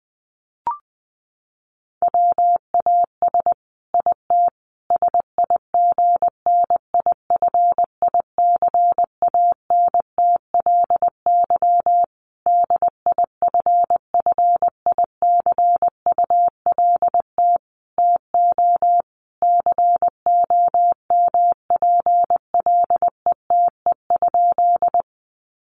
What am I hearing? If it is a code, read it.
Morse code, heard as a keyed tone, about 20 wpm, 700 Hz. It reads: WAS IT SIGNIFICANTLY DIFFICULT TO COMPLETE?
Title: Sentences from Top 1500 Words 20wpm